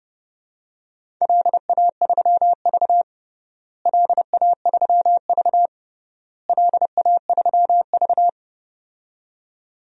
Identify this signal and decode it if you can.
Morse code, heard as a keyed tone, about 32 wpm, 700 Hz. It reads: LA3V LA3V LA3V
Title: DXCC-100 Call Signs - 4 Characters - Repeated 3x - 30wpm